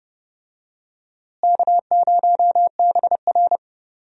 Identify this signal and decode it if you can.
Morse code, heard as a keyed tone, about 30 wpm, 700 Hz. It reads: K0BR
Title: USA Call Signs - 4 Characters - Repeated 2x - 30wpm